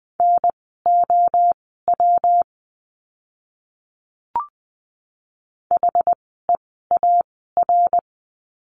Morse code, heard as a keyed tone, about 20 wpm, spaced as for 14 wpm, 700 Hz. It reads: NOW HEAR
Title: 206 Lesson 33f - Letter j - Words 20wpm